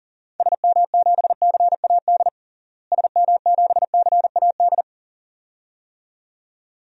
Morse code, heard as a keyed tone, about 40 wpm, 700 Hz. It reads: SM7CAD SM7CAD
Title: DXCC-100 Call Signs - 6 Characters - Repeated 2x - 40wpm